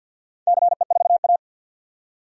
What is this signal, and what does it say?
Morse code, heard as a keyed tone, about 48 wpm, 700 Hz. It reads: CE4A
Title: DXCC-100 Call Signs - 4 Characters - 50wpm